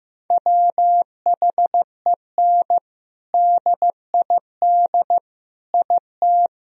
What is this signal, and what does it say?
Morse code, heard as a keyed tone, about 15 wpm, 700 Hz. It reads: WHEN DID IT
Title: Sentences from Top 1000 Words - Review Part 1 - 15wpm